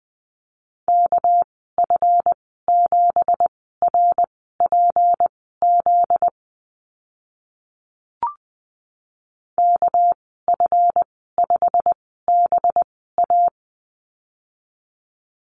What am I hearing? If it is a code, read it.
Morse code, heard as a keyed tone, about 20 wpm, spaced as for 14 wpm, 700 Hz. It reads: KF7RPZ KF5BA
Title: 245 Lesson 39g - Letter z - Call Signs 20wpm